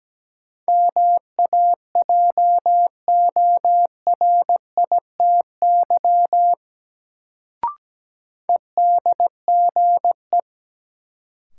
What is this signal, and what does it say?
Morse code, heard as a keyed tone, about 17 wpm, 700 Hz. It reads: MAJORITY EDGE
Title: Sets of 3 Words - New Word Drill - 17wpm